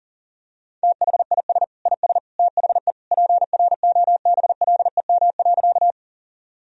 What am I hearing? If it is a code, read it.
Morse code, heard as a keyed tone, about 40 wpm, 700 Hz. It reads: THIS IS THE PROBLEM.
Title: Sentences from Top 1000 Words - Review Part 2 - 40wpm